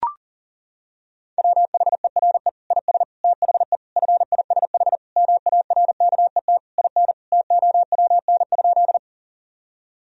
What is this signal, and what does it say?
Morse code, heard as a keyed tone, about 40 wpm, 700 Hz. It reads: WHERE IS THE FISH MARKET IN TOWN?
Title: Sentences from Top 1000 Words - Review Part 3 - 40wpm